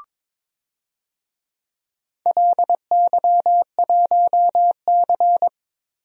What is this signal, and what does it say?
Morse code, heard as a keyed tone, about 22 wpm, 700 Hz. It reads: LY1C
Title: DXCC-100 Call Signs - 4 Characters - Repeated 3x - 22wpm